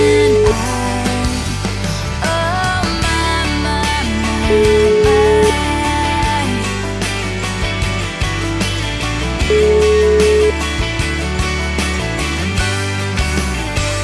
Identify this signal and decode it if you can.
Ringback tone (Europe, 425 Hz), with noise.